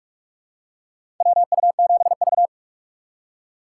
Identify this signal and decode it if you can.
Morse code, heard as a keyed tone, about 46 wpm, 700 Hz. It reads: WU7V
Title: USA Call Signs - 4 Characters - 6x Character Spacing - 45wpm